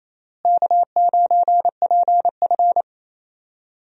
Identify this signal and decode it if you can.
Morse code, heard as a keyed tone, about 28 wpm, 700 Hz. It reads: K9PF
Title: USA Call Signs - 4 Characters - Repeated 3x - 28wpm